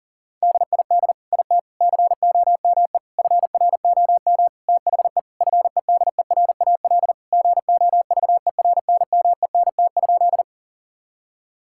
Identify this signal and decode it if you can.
Morse code, heard as a keyed tone, about 37 wpm, 700 Hz. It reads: DID IT COME FROM THE FEDERAL GOVERNMENT?
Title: Sentences from Top 1000 Words - Review Part 5 - 40wpm